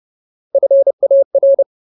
Morse code, heard as a keyed tone, about 30 wpm, 550 Hz. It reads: FAR
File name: Ultimate Word List / 30WPM Ultimate